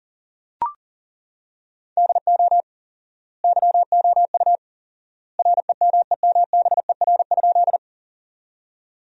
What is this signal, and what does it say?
Morse code, heard as a keyed tone, about 40 wpm, 700 Hz. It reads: DO YOU REMEMBER?
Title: 4x Word-Spacing - Sentences from Top 400 Words - 40wpm